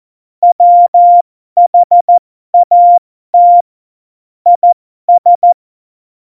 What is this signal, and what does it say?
Morse code, heard as a keyed tone, about 14 wpm, spaced as for 12 wpm, 700 Hz. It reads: WHAT IS
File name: Morsekurs / Witze